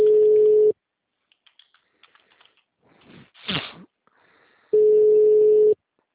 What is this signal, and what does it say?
Ringback tone (Europe, 425 Hz).